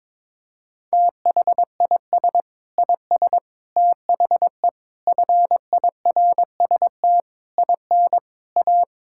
Morse code, heard as a keyed tone, about 22 wpm, 700 Hz. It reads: THIS IS THE FIRST IN A
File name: practice / 22wpm / Sets of 8 Words - 22wpm